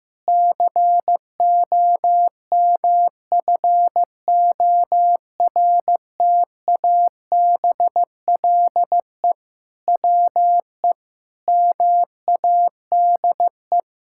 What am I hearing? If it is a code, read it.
Morse code, heard as a keyed tone, about 15 wpm, 700 Hz. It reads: COMFORTABLE WE MADE
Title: Sentences from Top 1600 Words 15wpm